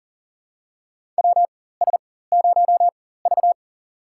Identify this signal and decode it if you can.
Morse code, heard as a keyed tone, about 40 wpm, spaced as for 19 wpm, 700 Hz. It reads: WS0V
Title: USA Call Signs - 4 Characters - 4x Character Spacing - 40wpm